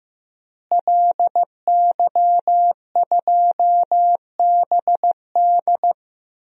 Morse code, heard as a keyed tone, about 15 wpm, 700 Hz. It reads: LY2BD
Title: DXCC-100 Call Signs - 5 Characters - Repeated 3x - 15wpm